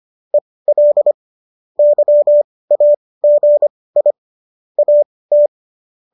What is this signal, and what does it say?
Morse code, heard as a keyed tone, about 25 wpm, spaced as for 18 wpm, 590 Hz. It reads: EL YAGI AT
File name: qso205 18